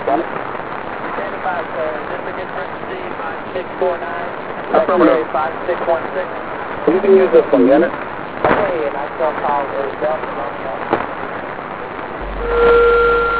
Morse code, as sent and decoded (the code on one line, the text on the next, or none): none